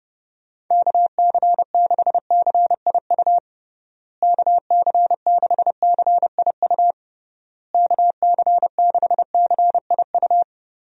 Morse code, heard as a keyed tone, about 30 wpm, 700 Hz. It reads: KC6CIU KC6CIU KC6CIU
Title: USA Call Signs - 6 Characters - Repeated 3x - 30wpm